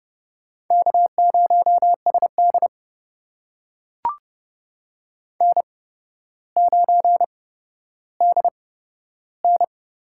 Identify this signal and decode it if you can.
Morse code, heard as a keyed tone, about 30 wpm, spaced as for 13 wpm, 700 Hz. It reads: K0SD N9DN
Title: USA Call Signs - 4 Characters - 8x Character Spacing - 30wpm